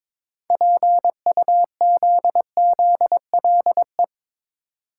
Morse code, heard as a keyed tone, about 22 wpm, 700 Hz. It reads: PUZZLE
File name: practice / 22wpm / Difficult Words with Hints - 22wpm